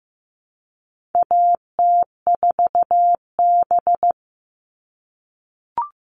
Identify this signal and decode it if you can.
Morse code, heard as a keyed tone, about 15 wpm, 700 Hz. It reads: AT4B
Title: All Cut Numbers - 4-Digit 15wpm